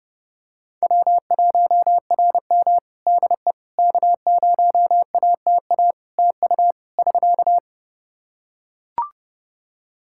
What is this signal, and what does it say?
Morse code, heard as a keyed tone, about 28 wpm, 700 Hz. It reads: W1RM DE K0ATA TU <SK>
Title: Practice QSO - Exchange 4 - Ending - 30wpm